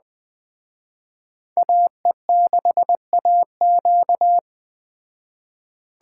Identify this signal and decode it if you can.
Morse code, heard as a keyed tone, about 21 wpm, 700 Hz. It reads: AE6AQ
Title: Speed Racing - USA 2x2 Call Signs 17wpm